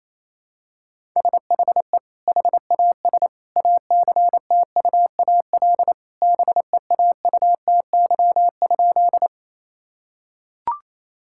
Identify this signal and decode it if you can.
Morse code, heard as a keyed tone, about 28 wpm, 700 Hz. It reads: SHE HAS ACTUAL BEAUTY?
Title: Sentences from Top 1000 Words - Review Part 5 - 28wpm